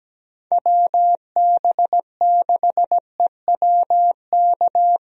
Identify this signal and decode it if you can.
Morse code, heard as a keyed tone, about 17 wpm, 700 Hz. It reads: WB6EWK
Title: USA Call Signs - 6 Characters - Repeated 3x - 17wpm